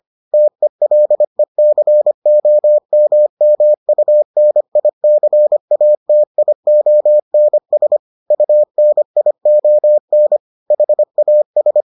Morse code, heard as a keyed tone, about 25 wpm, 600 Hz. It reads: TELECOMMUNICATIONS UNION HAS